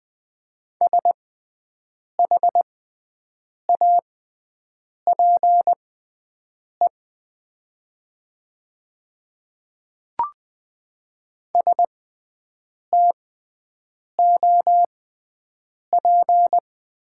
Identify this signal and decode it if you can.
Morse code, heard as a keyed tone, about 20 wpm, spaced as for 7 wpm, 700 Hz. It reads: SHAPE STOP
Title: 6x Character Spacing - Top 500 Words - 20wpm